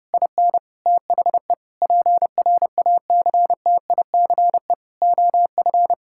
Morse code, heard as a keyed tone, about 32 wpm, 700 Hz. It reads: IN THE PRACTICE OF
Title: Sentences from Top 1000 Words - Review Part 6 - 30wpm